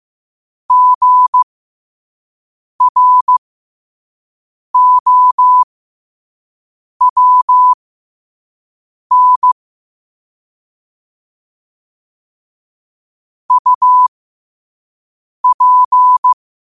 Morse code, heard as a keyed tone, about 15 wpm, spaced as for 5 wpm, 1000 Hz. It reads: GROWN UP
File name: qotdfiles / 5wpm